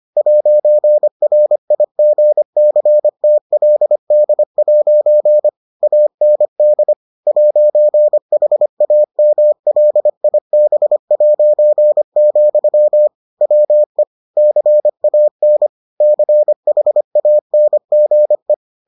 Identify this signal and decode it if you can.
Morse code, heard as a keyed tone, about 25 wpm, 600 Hz. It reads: 'RIGCTLD' AND 'HAMLIB', WE CAN CHANGE